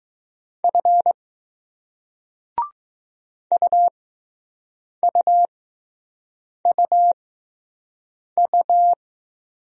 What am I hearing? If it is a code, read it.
Morse code, heard as a keyed tone, about 20 wpm, spaced as for 6 wpm, 700 Hz. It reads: F UUUU